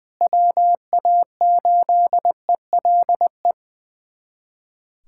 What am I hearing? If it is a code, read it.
Morse code, heard as a keyed tone, about 20 wpm, 700 Hz. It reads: WA8ELE
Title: USA Callsign DE Callsign - 6 Characters - 20wpm